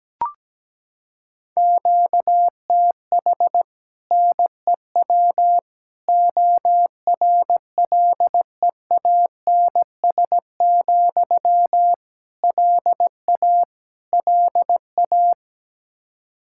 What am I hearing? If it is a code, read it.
Morse code, heard as a keyed tone, about 17 wpm, 700 Hz. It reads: QTH NEW ORLEANS, LA LA
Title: Practice QSO - Exchange 1 - Beginning - 17wpm